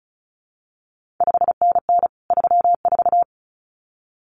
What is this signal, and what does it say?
Morse code, heard as a keyed tone, about 36 wpm, 700 Hz. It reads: 5NN 34
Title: CQ WW DX Contest Exchange Encore - 35wpm